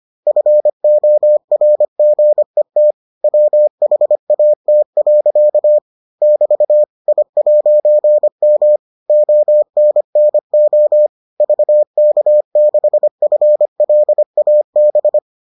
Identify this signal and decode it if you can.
Morse code, heard as a keyed tone, about 25 wpm, 600 Hz. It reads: FORGET WHAT. <BT> I'M ONNO VK6FLAB